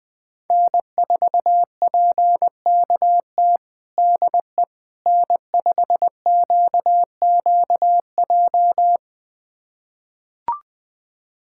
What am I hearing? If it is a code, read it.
Morse code, heard as a keyed tone, about 20 wpm, 700 Hz. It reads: N4PKT DE N5QQJ
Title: USA Callsign DE Callsign - 5 Characters - 20wpm